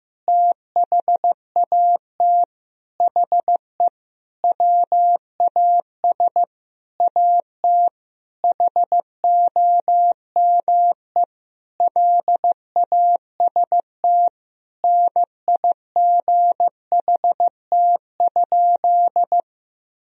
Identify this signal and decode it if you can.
Morse code, heard as a keyed tone, about 15 wpm, 700 Hz. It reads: THAT HE WAS AT HOME LAST NIGHT?
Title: Sentences from Top 1000 Words - Review Part 2 - 15wpm